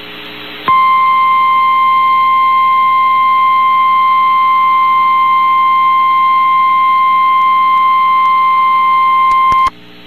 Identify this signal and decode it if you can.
Emergency Alert System (SAME) tones (the 1050 Hz attention tone). It is heard from 0.6 to 9.7 s.
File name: eas received alerts / L1 19498 audio